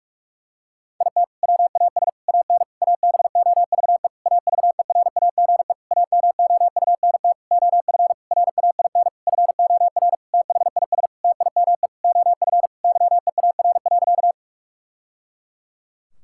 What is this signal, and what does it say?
Morse code, heard as a keyed tone, about 46 wpm, 700 Hz. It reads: IT WAS AN ABOVE AVERAGE AMOUNT OF RAIN FOR THIS TIME OF YEAR.
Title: Sentences from Top 1000 Words - Review Part 6 - 45wpm